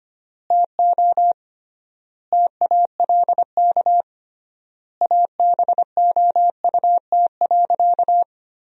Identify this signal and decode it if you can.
Morse code, heard as a keyed tone, about 25 wpm, 700 Hz. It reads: TO TALK ABOUT.
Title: 3x Word-Spacing - Sentences from Top 400 Words - 25wpm